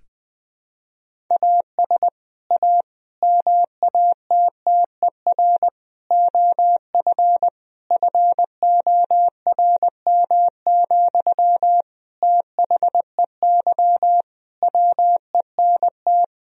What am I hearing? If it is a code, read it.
Morse code, heard as a keyed tone, about 21 wpm, 700 Hz. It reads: AS A MATTER OF FORM, THEY WENT
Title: Sentences from Top 1000 Words - Review Part 1 - 20wpm